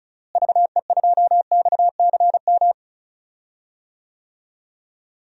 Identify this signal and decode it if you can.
Morse code, heard as a keyed tone, about 36 wpm, 700 Hz. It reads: VE2XCM
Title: DXCC-100 Call Signs - 6 Characters - Repeated 2x - 35wpm